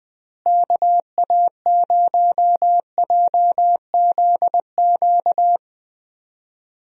Morse code, heard as a keyed tone, about 20 wpm, 700 Hz. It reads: KA0JZQ
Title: USA Call Signs - 6 Characters - Repeated 3x - 20wpm